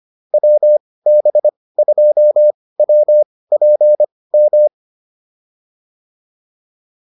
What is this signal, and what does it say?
Morse code, heard as a keyed tone, about 25 wpm, spaced as for 18 wpm, 600 Hz. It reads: WB2WPM
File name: QSO106 18